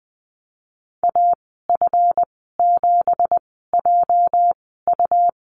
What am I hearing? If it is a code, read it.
Morse code, heard as a keyed tone, about 19 wpm, spaced as for 14 wpm, 700 Hz. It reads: AF7JU
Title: 213 Lesson 34g - Number 8 - Call Signs 20wpm